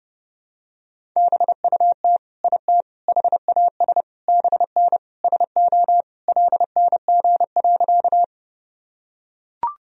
Morse code, heard as a keyed tone, about 30 wpm, 700 Hz. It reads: BUT IT HAS BN SO LNG.
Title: Sentences from Top 100 Words and Abbreviations 30wpm